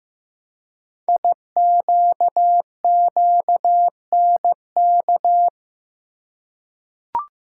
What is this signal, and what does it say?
Morse code, heard as a keyed tone, about 15 wpm, 700 Hz. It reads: IQQNK
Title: Code Groups - 5 Letters - 15wpm